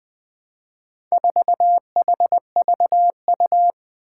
Morse code, heard as a keyed tone, about 20 wpm, 700 Hz. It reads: 4HVU
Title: Four Challenging Letter-Number Combinations 20wpm